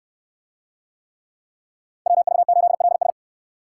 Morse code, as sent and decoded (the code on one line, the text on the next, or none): .-- ...- ---.. .-.. ...
WV8LS